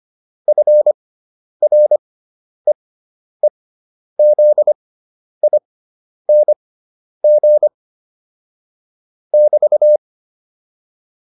Morse code, heard as a keyed tone, about 25 wpm, spaced as for 10 wpm, 600 Hz. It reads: FREEZING <BT>
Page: intermediate-practice-files